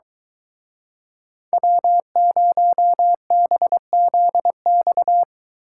Morse code, heard as a keyed tone, about 20 wpm, 700 Hz. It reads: W0BZX
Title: Speed Racing - USA 1x3 Call Signs 17wpm